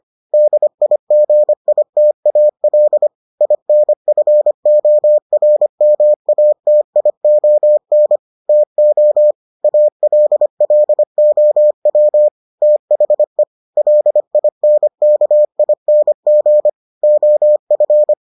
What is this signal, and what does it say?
Morse code, heard as a keyed tone, about 25 wpm, 600 Hz. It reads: DIGITAL INFORMATION TO ALLOW THE LINKING OF